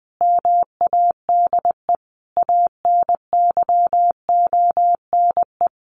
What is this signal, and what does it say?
Morse code, heard as a keyed tone, about 20 wpm, 700 Hz. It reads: MADE ANYONE
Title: Sentences from Top 1100 Words 20wpm